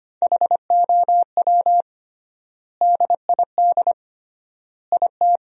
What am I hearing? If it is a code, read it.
Morse code, heard as a keyed tone, about 25 wpm, 700 Hz. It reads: HOW DID IT
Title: 3x Word-Spacing - Sentences from Top 300 Words - 25wpm